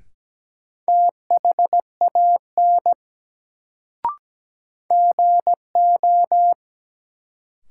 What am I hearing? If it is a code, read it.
Morse code, heard as a keyed tone, about 17 wpm, 700 Hz. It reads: THAN GO
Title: Sentences from Top 100 Words - New Word Drill - 17wpm